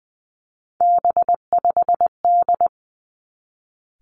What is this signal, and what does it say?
Morse code, heard as a keyed tone, about 20 wpm, 700 Hz. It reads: B5D
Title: Three Challenging Letter-Number Combinations 20wpm